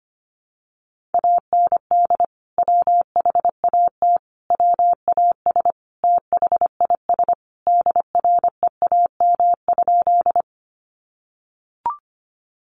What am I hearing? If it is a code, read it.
Morse code, heard as a keyed tone, about 25 wpm, 700 Hz. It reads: AND WHAT WAS THIS DREAM?
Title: Sentences from Top 1500 Words 25wpm